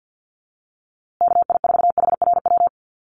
Morse code, heard as a keyed tone, about 50 wpm, 700 Hz. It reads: KI4HRW
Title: USA Call Signs Repeated 3x 50wpm